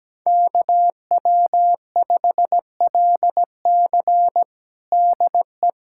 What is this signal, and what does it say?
Morse code, heard as a keyed tone, about 17 wpm, 700 Hz. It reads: KW5LC DE
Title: USA Callsign DE Callsign - 5 Characters - 17wpm